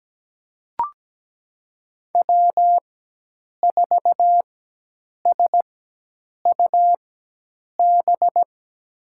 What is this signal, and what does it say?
Morse code, heard as a keyed tone, about 17 wpm, spaced as for 8 wpm, 700 Hz. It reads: W4SUB
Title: USA Call Signs - 5 Characters - 4x Character Spacing - 17wpm